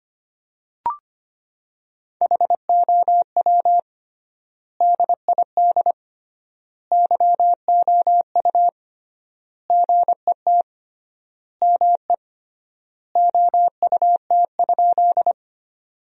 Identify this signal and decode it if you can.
Morse code, heard as a keyed tone, about 25 wpm, 700 Hz. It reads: HOW DID YOU GET ME OUT?
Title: 3x Word-Spacing - Sentences from Top 200 Words - 25wpm